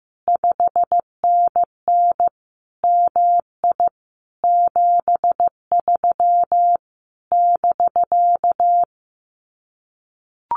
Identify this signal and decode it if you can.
Morse code, heard as a keyed tone, about 15 wpm, 700 Hz. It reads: 5NN MI 73 <BK>
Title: Common QSO Phrases Encore 15wpm